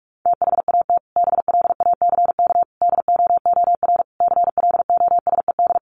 Morse code, heard as a keyed tone, about 44 wpm, 700 Hz. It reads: THAT BLACK DOOR CLOSED
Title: Sentences from Top 1400 Words 45wpm